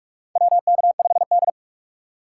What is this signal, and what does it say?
Morse code, heard as a keyed tone, about 46 wpm, 700 Hz. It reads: WK5D
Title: USA Call Signs - 3x Character Spacing - 45wpm